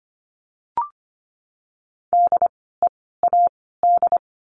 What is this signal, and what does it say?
Morse code, heard as a keyed tone, about 25 wpm, 700 Hz. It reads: D E A D
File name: course / 25:10wpm / 094 Lesson 16f - Letter c - Words 25wpm